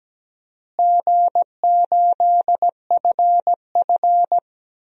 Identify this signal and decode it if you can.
Morse code, heard as a keyed tone, about 17 wpm, 700 Hz. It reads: G8FF
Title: DXCC-100 Call Signs - 4 Characters - Repeated 2x - 17wpm